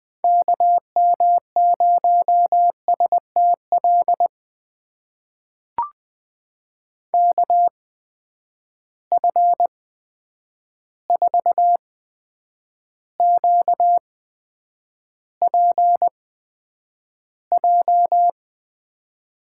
Morse code, heard as a keyed tone, about 21 wpm, spaced as for 9 wpm, 700 Hz. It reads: KM0STL KF4QPJ
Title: USA Call Signs - 6 Characters - 8x Character Spacing - 20wpm